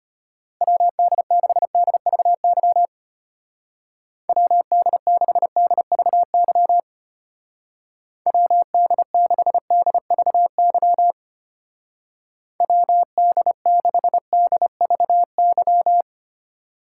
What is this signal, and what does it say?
Morse code, heard as a keyed tone, about 31 wpm, 700 Hz. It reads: WD6DVY WD6DVY WD6DVY WD6DVY
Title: Speed Racing - USA 2x3 Call Signs 25wpm